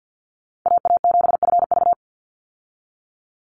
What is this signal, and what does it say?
Morse code, heard as a keyed tone, about 50 wpm, 700 Hz. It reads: AA7FV
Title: Call Signs - 50wpm